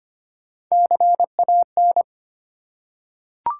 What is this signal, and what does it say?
Morse code, heard as a keyed tone, about 25 wpm, 700 Hz. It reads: CAN
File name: Top 90 QSO Elements - 3x Character Spacing - 25wpm